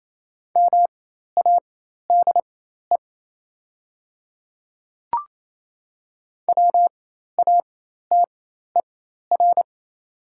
Morse code, heard as a keyed tone, about 27 wpm, spaced as for 13 wpm, 690 Hz. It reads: MADE WATER
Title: 4x Character Spacing - Top 200 Words - 28wpm